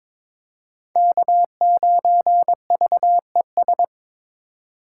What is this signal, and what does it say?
Morse code, heard as a keyed tone, about 22 wpm, 700 Hz. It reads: K9VES
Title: USA Call Signs - 5 Characters - 4x Character Spacing - 22wpm